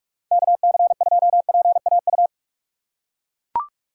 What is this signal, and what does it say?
Morse code, heard as a keyed tone, about 41 wpm, 700 Hz. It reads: KC2PAU
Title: USA Call Signs - 6 Characters - 2x Character Spacing - 45wpm